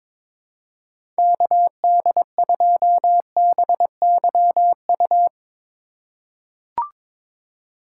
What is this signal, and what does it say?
Morse code, heard as a keyed tone, about 22 wpm, 700 Hz. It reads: KD2BYU
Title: USA Call Signs - 4x Character Spacing - 22wpm